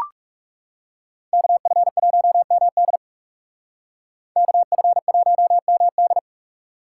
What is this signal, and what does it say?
Morse code, heard as a keyed tone, about 38 wpm, 700 Hz. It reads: KF1MD KF1MD